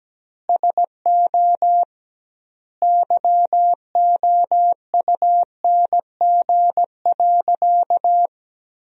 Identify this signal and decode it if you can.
Morse code, heard as a keyed tone, about 17 wpm, 700 Hz. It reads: SO YOUNG.
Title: 2x Word-Spacing - Sentences from Top 400 Words - 17wpm